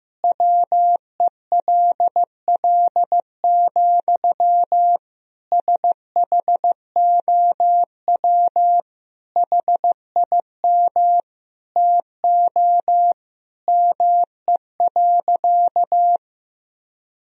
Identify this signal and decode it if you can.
Morse code, heard as a keyed tone, about 15 wpm, 700 Hz. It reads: WELL, SHOW HIM TO ME.
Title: Sentences from Top 200 Words and Abbreviations 15wpm